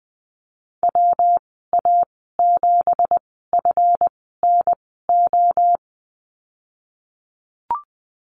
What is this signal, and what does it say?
Morse code, heard as a keyed tone, about 20 wpm, spaced as for 14 wpm, 700 Hz. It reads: WA7FNO
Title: 186 Lesson 30g - Letter b - Call Signs 20wpm